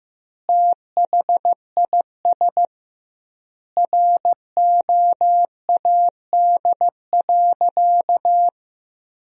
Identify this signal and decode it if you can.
Morse code, heard as a keyed tone, about 15 wpm, 700 Hz. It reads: THIS ROAD.
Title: 2x Word-Spacing - Sentences from Top 400 Words - 15wpm